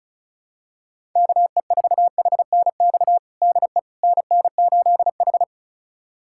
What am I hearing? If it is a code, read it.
Morse code, heard as a keyed tone, about 34 wpm, 700 Hz. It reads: KE4HNX DE NN8H
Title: USA Callsign DE Callsign 35wpm